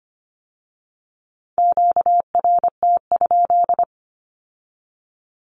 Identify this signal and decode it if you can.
Morse code, heard as a keyed tone, about 25 wpm, 700 Hz. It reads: QRT?
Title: Q-Codes with Spoken Meaning - 25wpm